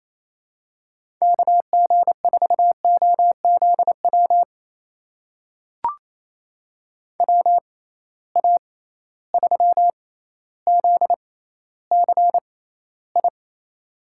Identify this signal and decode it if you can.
Morse code, heard as a keyed tone, about 28 wpm, spaced as for 14 wpm, 700 Hz. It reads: KG4OZW WA3ZCI
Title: USA Call Signs - 6x Character Spacing - 28wpm